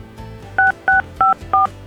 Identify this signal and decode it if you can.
DTMF, heard as telephone touch tones, tones of about 140 ms. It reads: 6654